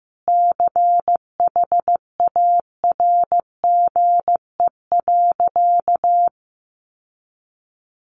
Morse code, heard as a keyed tone, about 15 wpm, 700 Hz. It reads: CHARGE.
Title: Sentences from Top 900 Words 15wpm